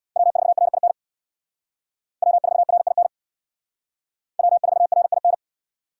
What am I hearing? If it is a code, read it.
Morse code, heard as a keyed tone, about 63 wpm, 700 Hz. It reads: W4RIN W4RIN W4RIN